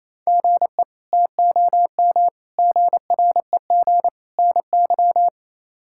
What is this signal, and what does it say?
Morse code, heard as a keyed tone, about 28 wpm, 700 Hz. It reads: GE TOM GREG NY
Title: K1USN SST QSO Practice - 28wpm